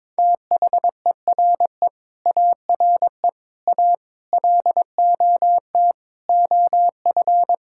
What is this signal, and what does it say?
Morse code, heard as a keyed tone, about 22 wpm, 700 Hz. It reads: THERE ARE A LOT OF
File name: practice / 22wpm / Sets of 6 Words - 22wpm